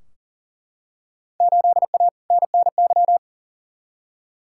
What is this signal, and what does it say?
Morse code, heard as a keyed tone, about 40 wpm, 700 Hz. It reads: 8A NNY
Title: Field Day Exchanges Encore - 40wpm